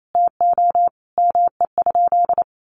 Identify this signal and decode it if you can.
Morse code, heard as a keyed tone, about 28 wpm, 700 Hz. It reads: TO ME?
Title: Sentences from Top 200 Words 28wpm